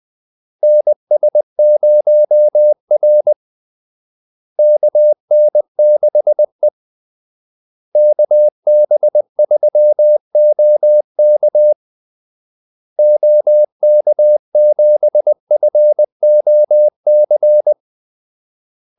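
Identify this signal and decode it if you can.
Morse code, heard as a keyed tone, about 20 wpm, 600 Hz. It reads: NS0R KN6E KB3OK OK7FOC
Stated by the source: Morse call-sign practice at 20 wpm.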